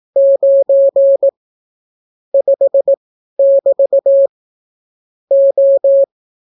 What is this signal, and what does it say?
Morse code, heard as a keyed tone, about 18 wpm, spaced as for 12 wpm, 550 Hz. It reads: 9 5<BT> O